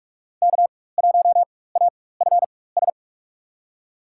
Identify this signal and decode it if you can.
Morse code, heard as a keyed tone, about 44 wpm, spaced as for 21 wpm, 700 Hz. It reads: K1AFS
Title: USA Call Signs - 5 Characters - 4x Character Spacing - 45wpm